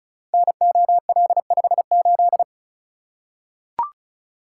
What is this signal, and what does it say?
Morse code, heard as a keyed tone, about 36 wpm, 700 Hz. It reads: NOL58
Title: Code Groups - 5 Letter:Numbers - 35wpm